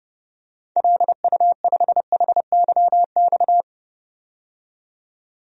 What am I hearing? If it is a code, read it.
Morse code, heard as a keyed tone, about 30 wpm, 700 Hz. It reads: LU5HYX
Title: DXCC-100 Call Signs - 6 Characters - 30wpm